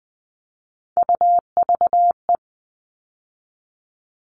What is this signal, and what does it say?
Morse code, heard as a keyed tone, about 20 wpm, 700 Hz. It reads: UVE
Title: All Cut Numbers - 3-Digit 20wpm